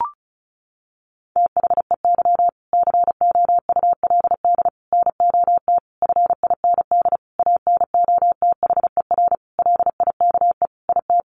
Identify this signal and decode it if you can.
Morse code, heard as a keyed tone, about 34 wpm, 700 Hz. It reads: THEY COULD NOT FIND ANOTHER LIKE IT
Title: Sentences from Top 1400 Words 35wpm